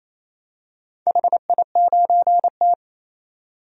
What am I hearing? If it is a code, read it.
Morse code, heard as a keyed tone, about 28 wpm, 700 Hz. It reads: HI9T
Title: DXCC-100 Call Signs - 4 Characters - 28wpm